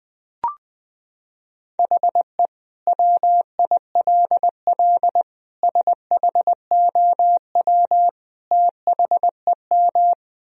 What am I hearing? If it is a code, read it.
Morse code, heard as a keyed tone, about 20 wpm, 700 Hz. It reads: HE WILL SHOW THEM
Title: Sentences from Top 1000 Words - Review Part 1 - 20wpm